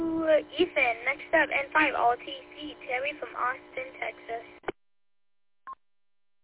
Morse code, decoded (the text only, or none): none